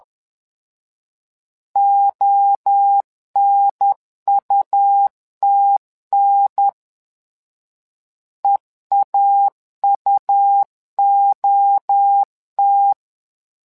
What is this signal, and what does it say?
Morse code, heard as a keyed tone, about 10 wpm, 800 Hz. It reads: ONUTN EAUOT
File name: LEZIONI / 3 MWRO / 3 MWRO 10wpm 2148